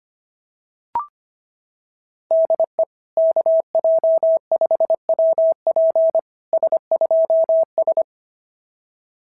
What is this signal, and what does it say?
Morse code, heard as a keyed tone, about 26 wpm, 650 Hz. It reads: DE KJ5WP S2S
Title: SOTA QSO Practice 25wpm